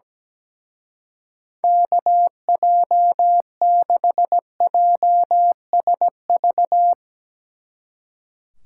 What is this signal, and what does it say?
Morse code, heard as a keyed tone, about 17 wpm, 700 Hz. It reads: KJ6JSV
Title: Speed Racing - USA 2x3 Call Signs 17wpm